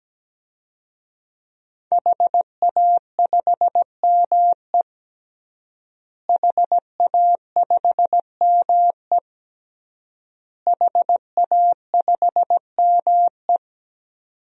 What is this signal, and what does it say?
Morse code, heard as a keyed tone, about 17 wpm, 700 Hz. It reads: HA5ME HA5ME HA5ME
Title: DXCC-100 Call Signs - 5 Characters - Repeated 3x - 17wpm